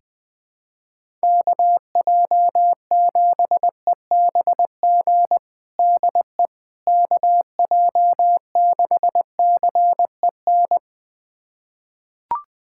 Morse code, heard as a keyed tone, about 20 wpm, 700 Hz. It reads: KJ7EBG DE KJ6CEN
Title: USA Callsign DE Callsign - 6 Characters - 20wpm